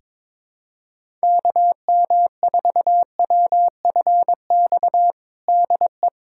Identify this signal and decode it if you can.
Morse code, heard as a keyed tone, about 22 wpm, 700 Hz. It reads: KM4WFX DE
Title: USA Callsign DE Callsign - 6 Characters - 22wpm